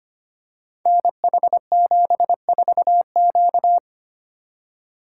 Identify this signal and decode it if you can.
Morse code, heard as a keyed tone, about 25 wpm, 700 Hz. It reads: NH74Q
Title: Code Groups - 5 Letter:Numbers - 25wpm